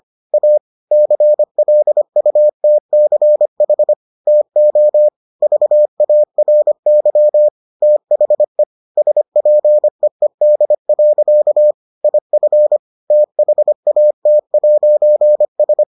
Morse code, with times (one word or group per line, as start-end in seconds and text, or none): 0.3-0.6 A
0.9-3.9 CLUTCH
4.3-5.1 TO
5.4-7.5 VARY
7.8-8.6 THE
9.0-11.7 SPEED.
12.0-12.8 IF
13.1-15.8 THAT'S